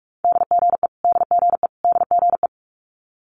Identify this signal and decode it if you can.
Morse code, heard as a keyed tone, about 46 wpm, 700 Hz. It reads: DGE DGE DGE
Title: Multi-letter Phonemes Repeated 3x 45wpm